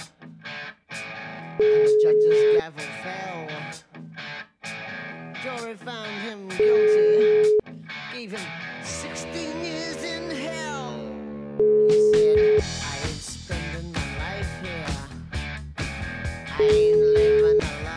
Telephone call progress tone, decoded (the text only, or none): ringback tone